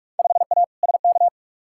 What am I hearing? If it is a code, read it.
Morse code, heard as a keyed tone, about 41 wpm, 700 Hz. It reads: 5A SK
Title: Field Day Exchanges Encore - 45wpm